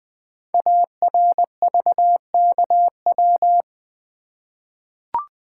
Morse code, heard as a keyed tone, about 20 wpm, 700 Hz. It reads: ARVKW
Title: Code Groups - 5 Letters - 20wpm